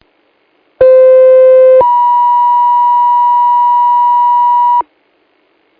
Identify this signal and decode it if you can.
Two-tone page, heard as a one-second tone, then a three-second tone, about 520 then 970 Hz. It is heard from 0.8 to 4.8 s.